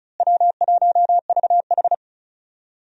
Morse code, heard as a keyed tone, about 36 wpm, 700 Hz. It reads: W1VH
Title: USA Call Signs - 4 Characters - Repeated 3x - 35wpm